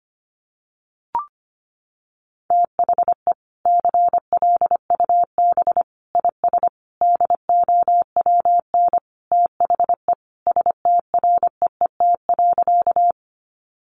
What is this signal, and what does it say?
Morse code, heard as a keyed tone, about 25 wpm, 700 Hz. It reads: THE CLUB IS DOWN THE STREET.
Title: Sentences from Top 1200 Words 25wpm